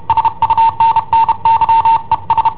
Morse code, heard as a keyed tone, about 31 wpm, 910 Hz. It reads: SUNNY ES
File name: i7ohp / cassetta i7ohp B